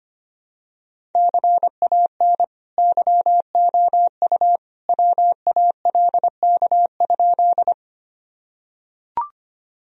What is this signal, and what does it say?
Morse code, heard as a keyed tone, about 25 wpm, 700 Hz. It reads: CAN YOU WALK?
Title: Sentences from Top 1000 Words - Review Part 1 - 25wpm